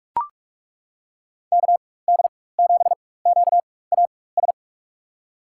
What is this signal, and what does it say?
Morse code, heard as a keyed tone, about 42 wpm, spaced as for 20 wpm, 700 Hz. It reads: KD7QAS
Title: USA Call Signs - 6 Characters - 4x Character Spacing - 45wpm